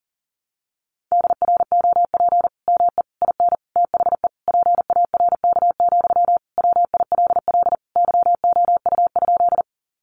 Morse code, heard as a keyed tone, about 40 wpm, 700 Hz. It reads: DROP ME IN THE PARK, WILL YOU?
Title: Sentences from Top 1400 Words 40wpm